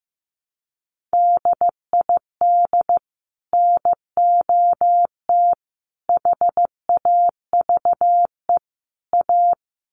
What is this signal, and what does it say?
Morse code, heard as a keyed tone, about 15 wpm, 700 Hz. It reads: DID NOT HAVE A
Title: Sets of 4 Words 15wpm